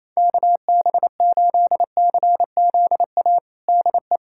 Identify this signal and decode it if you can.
Morse code, heard as a keyed tone, about 28 wpm, 700 Hz. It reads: KB8CZA DE
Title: USA Callsign DE Callsign 28wpm